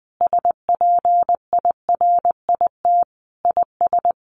Morse code, heard as a keyed tone, about 19 wpm, 700 Hz. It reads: SPIRIT IS
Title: Sentences from Top 1500 Words 20wpm